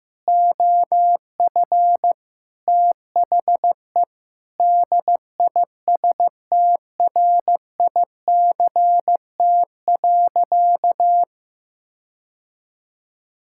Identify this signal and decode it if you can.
Morse code, heard as a keyed tone, about 15 wpm, 700 Hz. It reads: OF THE DISTRICT.
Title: Sentences from Top 1600 Words 15wpm